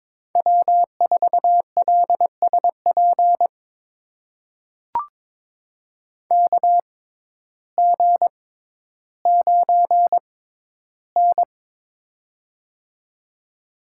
Morse code, heard as a keyed tone, about 22 wpm, spaced as for 12 wpm, 700 Hz. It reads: W4LSP KG9N
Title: USA Call Signs - 6x Character Spacing - 22wpm